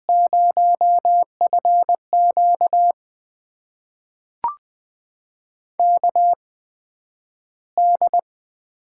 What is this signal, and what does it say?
Morse code, heard as a keyed tone, about 20 wpm, spaced as for 11 wpm, 700 Hz. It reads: 0FQ KD
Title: USA Call Signs - 8x Character Spacing - 20wpm